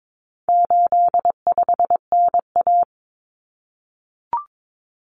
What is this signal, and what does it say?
Morse code, heard as a keyed tone, about 22 wpm, 700 Hz. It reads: 85NA
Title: Common Cut Numbers - 4-Digit 22wpm